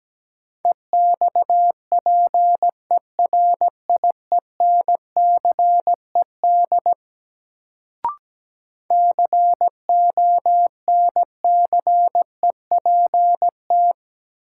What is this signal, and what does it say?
Morse code, heard as a keyed tone, about 17 wpm, 700 Hz. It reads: EXPERIENCED CONCEPT
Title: Sentences from Top 1100 Words - New Word Drill - 17wpm